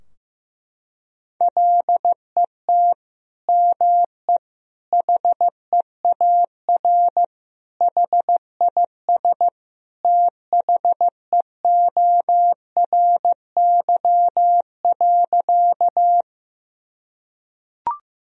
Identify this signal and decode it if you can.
Morse code, heard as a keyed tone, about 15 wpm, 700 Hz. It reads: LET ME HEAR HIS THEORY.
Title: Sentences from Top 1000 Words - Review Part 3 - 15wpm